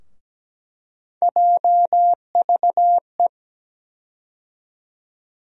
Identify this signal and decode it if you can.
Morse code, heard as a keyed tone, about 17 wpm, 700 Hz. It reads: JVE
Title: USA Call Sign Postfixes - 3 Letters 17wpm